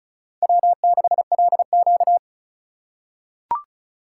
Morse code, heard as a keyed tone, about 36 wpm, 700 Hz. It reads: W6LQ
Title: USA Call Signs - 4 Characters - 3x Character Spacing - 35wpm